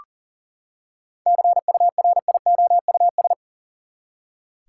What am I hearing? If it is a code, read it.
Morse code, heard as a keyed tone, about 37 wpm, 700 Hz. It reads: CURIOUS
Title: Seven Letter Words - 40wpm